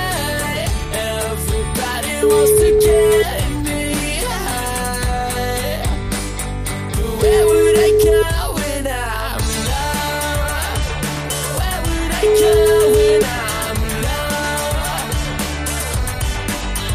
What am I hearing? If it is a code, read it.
Ringback tone (Europe, 425 Hz), with noise.